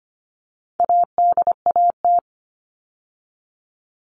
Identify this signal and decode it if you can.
Morse code, heard as a keyed tone, about 25 wpm, 700 Hz. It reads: ADAT
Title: All Cut Numbers - 4-Digit 25wpm